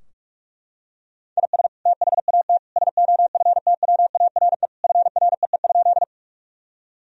Morse code, heard as a keyed tone, about 42 wpm, 700 Hz. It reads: IS THAT SOFTWARE FREE?
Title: Sentences from Top 1000 Words - Review Part 3 - 45wpm